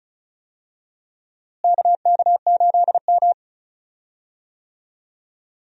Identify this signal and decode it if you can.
Morse code, heard as a keyed tone, about 36 wpm, 700 Hz. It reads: KK8M
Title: USA Call Signs - 4 Characters - Repeated 3x - 35wpm